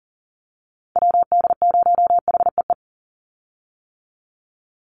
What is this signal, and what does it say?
Morse code, heard as a keyed tone, about 40 wpm, 700 Hz. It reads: WD0HEE
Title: USA Call Signs 40wpm